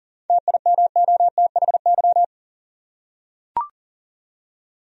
Morse code, heard as a keyed tone, about 40 wpm, 700 Hz. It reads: TIMOTHY
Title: USA Common Names - 40wpm